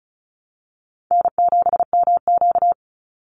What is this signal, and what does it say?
Morse code, heard as a keyed tone, about 34 wpm, 700 Hz. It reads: N7MQ
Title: Call Signs Repeated 3x - 35wpm